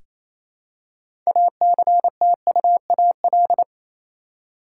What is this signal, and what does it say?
Morse code, heard as a keyed tone, about 28 wpm, 700 Hz. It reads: ACTUAL
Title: Six Letter Words - 28wpm